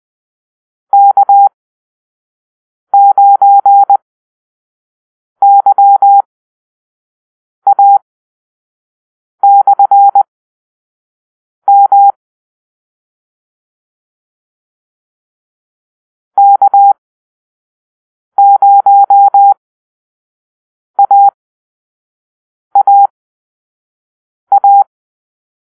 Morse code, heard as a keyed tone, about 20 wpm, spaced as for 5 wpm, 800 Hz. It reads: K9YA/M K0AAA
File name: arrl / Call-Sign Drill-4